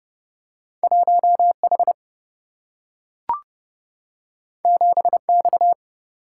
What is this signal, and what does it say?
Morse code, heard as a keyed tone, about 29 wpm, 700 Hz. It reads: 1 H 7 X
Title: Single-Letter Single-Number 30wpm